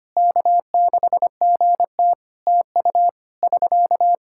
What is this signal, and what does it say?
Morse code, heard as a keyed tone, about 25 wpm, 700 Hz. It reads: K6GT TU <SK>
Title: Practice QSO - Exchange 4 - Ending - 25wpm